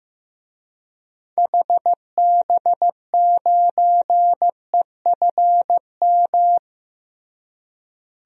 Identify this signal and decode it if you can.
Morse code, heard as a keyed tone, about 15 wpm, 700 Hz. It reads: HB9EFM
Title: DXCC-100 Call Signs - 6 Characters - Repeated 2x - 15wpm